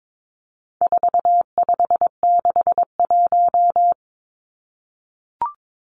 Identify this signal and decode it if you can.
Morse code, heard as a keyed tone, about 22 wpm, 700 Hz. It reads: 4561
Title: Numbers - 4-Digit 22wpm